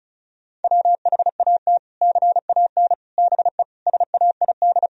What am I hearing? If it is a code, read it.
Morse code, heard as a keyed tone, about 36 wpm, 700 Hz. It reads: WHAT CAN BE SAID
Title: Sentences from Top 1000 Words - Review Part 4 - 35wpm